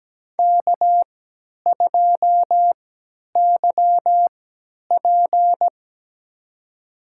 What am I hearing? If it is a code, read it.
Morse code, heard as a keyed tone, about 17 wpm, spaced as for 10 wpm, 700 Hz. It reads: K2YP
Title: USA Call Signs - 4 Characters - 3x Character Spacing - 17wpm